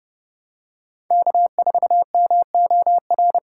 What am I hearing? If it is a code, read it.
Morse code, heard as a keyed tone, about 30 wpm, 700 Hz. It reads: K4MOR